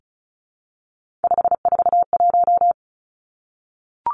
Morse code, heard as a keyed tone, about 36 wpm, 700 Hz. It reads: 541
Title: Three Challenging Letter-Number Combinations 35wpm